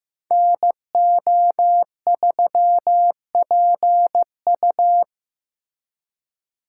Morse code, heard as a keyed tone, about 18 wpm, 700 Hz. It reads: NO3PU
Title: Speed Racing - USA 2x2 Call Signs 15wpm